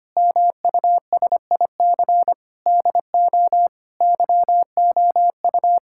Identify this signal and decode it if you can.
Morse code, heard as a keyed tone, about 25 wpm, 700 Hz. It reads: MUSIC DO YOU
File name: practice / 25wpm / Sentences with 3-Word Phrases - 25wpm